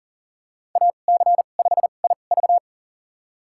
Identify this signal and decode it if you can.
Morse code, heard as a keyed tone, about 40 wpm, spaced as for 29 wpm, 700 Hz. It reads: AC5IV
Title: USA Call Signs - 5 Characters - 2x Character Spacing - 40wpm